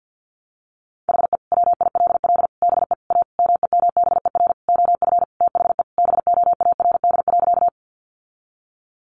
Morse code, heard as a keyed tone, about 50 wpm, 700 Hz. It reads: HE WILL BE A MEMBER OF THE BOARD.
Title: Sentences from Top 1100 Words 50wpm